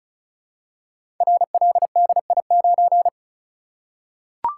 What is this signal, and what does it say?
Morse code, heard as a keyed tone, about 36 wpm, 700 Hz. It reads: RLDI9
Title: Code Groups - 5 Letter:Numbers - 35wpm